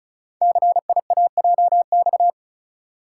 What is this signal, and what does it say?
Morse code, heard as a keyed tone, about 36 wpm, 690 Hz. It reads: CIAJX
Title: Code Groups - 5 Letters - 35wpm